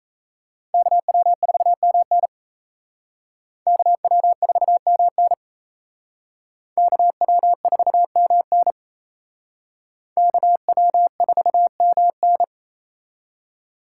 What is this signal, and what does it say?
Morse code, heard as a keyed tone, about 35 wpm, 700 Hz. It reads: KW4MN KW4MN KW4MN KW4MN
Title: Speed Racing - USA 2x2 Call Signs 28wpm